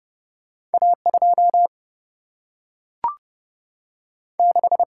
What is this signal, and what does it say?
Morse code, heard as a keyed tone, about 30 wpm, 700 Hz. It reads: A 2 6
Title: Single-Letter Single-Number 30wpm